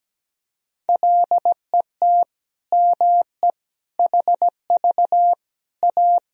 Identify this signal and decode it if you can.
Morse code, heard as a keyed tone, about 17 wpm, 700 Hz. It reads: LET ME HV A
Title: Sentences from Top 300 Words and Abbreviations 17wpm